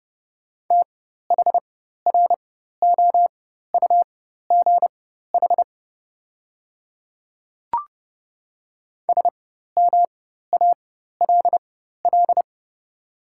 Morse code, heard as a keyed tone, about 30 wpm, spaced as for 14 wpm, 700 Hz. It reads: THROUGH SMALL
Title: 4x Character Spacing - Top 200 Words - 30wpm